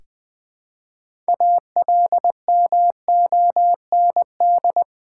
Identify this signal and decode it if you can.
Morse code, heard as a keyed tone, about 20 wpm, 700 Hz. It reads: ALMOND
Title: Difficult Words with Hints - 20wpm